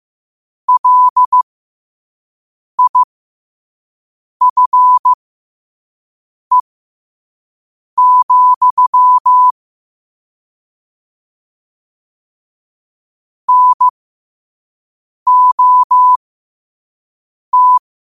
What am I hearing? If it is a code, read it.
Morse code, heard as a keyed tone, about 15 wpm, spaced as for 5 wpm, 1000 Hz. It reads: LIFE, NOT
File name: qotdfiles / 5wpm